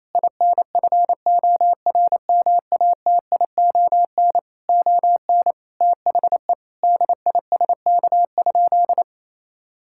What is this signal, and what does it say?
Morse code, heard as a keyed tone, about 28 wpm, 700 Hz. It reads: INFORMATION ON THE DISK?
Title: Sentences from Top 1000 Words - Review Part 6 - 28wpm